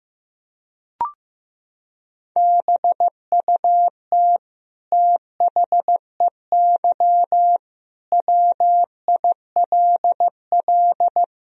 Morse code, heard as a keyed tone, about 15 wpm, 700 Hz. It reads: BUT THEY WILL
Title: Sentences from Top 1000 Words - Review Part 1 - 15wpm